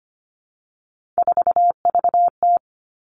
Morse code, heard as a keyed tone, about 25 wpm, 700 Hz. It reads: 4VT
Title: All Cut Numbers - 3-Digit 25wpm